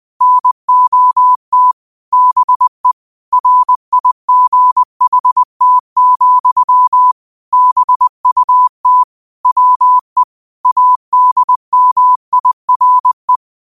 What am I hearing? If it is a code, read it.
Morse code, heard as a keyed tone, about 20 wpm, 1000 Hz. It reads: NOT BE RIGHT, BUT WE ADMIRE
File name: qotdfiles / 20wpm